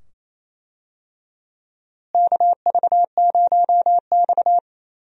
Morse code, heard as a keyed tone, about 28 wpm, 700 Hz. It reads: KV0X
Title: USA Call Signs - 4 Characters - Repeated 2x - 28wpm